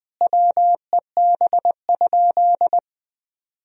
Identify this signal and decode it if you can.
Morse code, heard as a keyed tone, about 20 wpm, 700 Hz. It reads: WEB?
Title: Sentences from Top 1000 Words - Review Part 4 - 20wpm